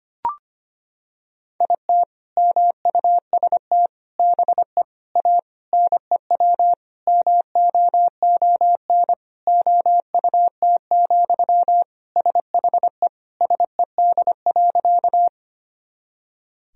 Morse code, heard as a keyed tone, about 25 wpm, 700 Hz. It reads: IT MUST BE A NEW MOON OUT, SHE SED.
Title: Sentences from Top 500 Words and Abbreviations 25wpm